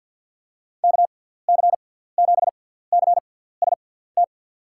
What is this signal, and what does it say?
Morse code, heard as a keyed tone, about 46 wpm, spaced as for 17 wpm, 700 Hz. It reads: KC7CST
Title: USA Call Signs - 6 Characters - 6x Character Spacing - 50wpm